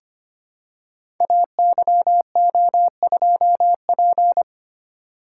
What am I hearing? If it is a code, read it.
Morse code, heard as a keyed tone, about 25 wpm, 700 Hz. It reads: AYO2P
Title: Code Groups - 5 Letter:Numbers - 25wpm